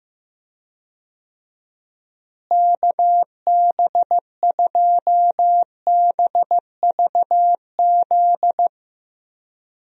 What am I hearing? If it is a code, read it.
Morse code, heard as a keyed tone, about 15 wpm, 700 Hz. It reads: KB2BVZ
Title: USA Call Signs - 6 Characters - Repeated 2x - 15wpm